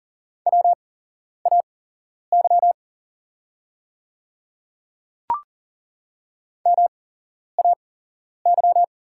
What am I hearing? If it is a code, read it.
Morse code, heard as a keyed tone, about 40 wpm, spaced as for 11 wpm, 690 Hz. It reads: WAY MAY
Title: 8x Character Spacing - Top 100 Words - 40wpm